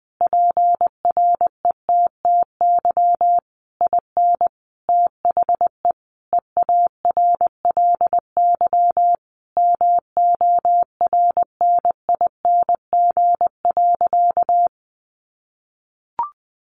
Morse code, heard as a keyed tone, about 19 wpm, 700 Hz. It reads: PRETTY IN THE EARLY MORNING.
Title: Sentences from Top 1300 Words 20wpm